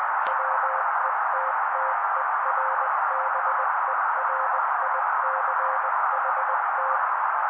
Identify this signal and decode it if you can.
Morse code, heard as a keyed tone, about 20 wpm, 530 Hz. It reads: WETTERBERICHT
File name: daily / 18wpm noise